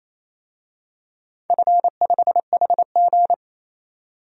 Morse code, heard as a keyed tone, about 28 wpm, 700 Hz. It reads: F5HG
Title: DXCC-100 Call Signs - 4 Characters - Repeated 2x - 28wpm